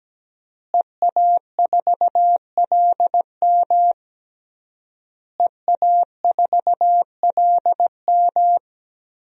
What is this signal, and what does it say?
Morse code, heard as a keyed tone, about 17 wpm, 700 Hz. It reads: EA4LM EA4LM
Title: DXCC-100 Call Signs - 5 Characters - Repeated 3x - 17wpm